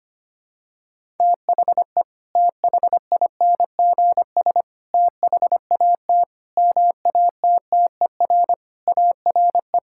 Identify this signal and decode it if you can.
Morse code, heard as a keyed tone, about 25 wpm, 700 Hz. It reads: THE THINGS THAT MATTER ARE
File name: practice / 25wpm / Sentences with 3-Word Phrases - 25wpm